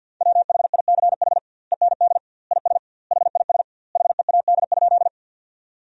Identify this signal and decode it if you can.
Morse code, heard as a keyed tone, about 50 wpm, 700 Hz. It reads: WHICH END IS HIS HEAD?
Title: 2x Word-Spacing - Sentences from Top 300 Words - 50wpm